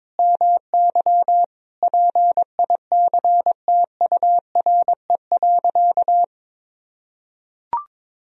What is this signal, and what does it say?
Morse code, heard as a keyed tone, about 22 wpm, 700 Hz. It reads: MY PICTURE.
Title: Sentences from Top 1000 Words - Review Part 1 - 22wpm